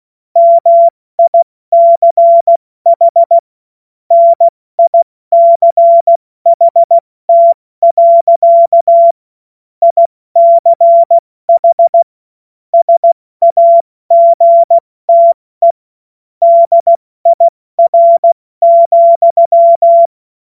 Morse code, heard as a keyed tone, about 16 wpm, 680 Hz. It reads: MICH NICHT. ICH SAGTE DIR,
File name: Morsekurs / Witze / Witz 08. November 2022 70 F80